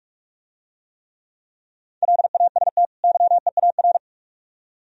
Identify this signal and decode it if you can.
Morse code, heard as a keyed tone, about 47 wpm, 700 Hz. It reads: LAST YEAR